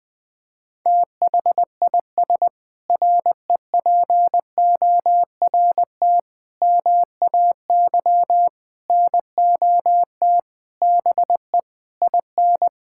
Morse code, heard as a keyed tone, about 19 wpm, 700 Hz. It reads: THIS REPORT MAY NOT BE IN
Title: Sentences with 7-Word Phrases - 20wpm